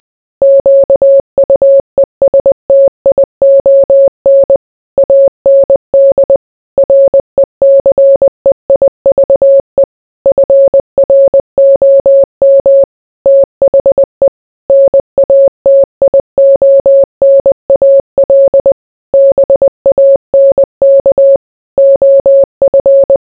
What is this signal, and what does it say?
Morse code, heard as a keyed tone, about 20 wpm, 550 Hz. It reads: QUESTION AND RECEIVE FROM THE NATIONAL BANK OF